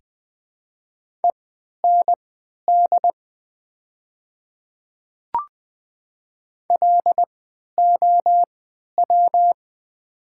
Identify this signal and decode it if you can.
Morse code, heard as a keyed tone, about 20 wpm, spaced as for 11 wpm, 700 Hz. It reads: END LOW
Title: 3x Character Spacing - Top 200 Words - 20wpm